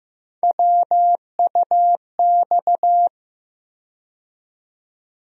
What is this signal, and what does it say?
Morse code, heard as a keyed tone, about 15 wpm, 700 Hz. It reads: WUX
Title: USA Call Sign Postfixes - 3 Letters 15wpm